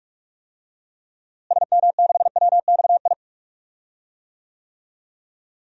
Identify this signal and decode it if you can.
Morse code, heard as a keyed tone, about 46 wpm, 700 Hz. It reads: SM6WXI
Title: DXCC-100 Call Signs - 6 Characters - Repeated 2x - 45wpm